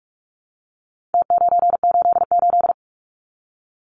Morse code, heard as a keyed tone, about 44 wpm, 700 Hz. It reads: T988
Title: Numbers - Contest Style - 4-Digit 45wpm